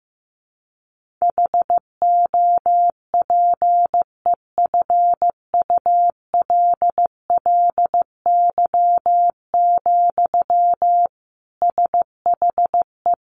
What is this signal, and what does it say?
Morse code, heard as a keyed tone, about 15 wpm, 700 Hz. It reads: HOPEFULLY, SHE
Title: Sentences from Top 1500 Words 15wpm